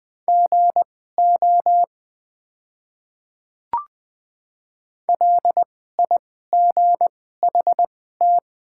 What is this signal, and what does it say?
Morse code, heard as a keyed tone, about 20 wpm, spaced as for 14 wpm, 700 Hz. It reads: GO LIGHT
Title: 2x Character Spacing - Top 200 Words - 20wpm